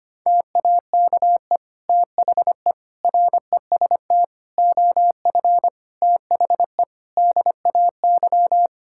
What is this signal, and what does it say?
Morse code, heard as a keyed tone, about 25 wpm, 700 Hz. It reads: TAKE THE REST OF THE DAY
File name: practice / 25wpm / Sentences with 2-Word Phrases - 25wpm